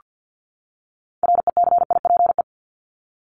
Morse code, heard as a keyed tone, about 48 wpm, 700 Hz. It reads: RECIPE
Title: Top 1001-1250 Words 50wpm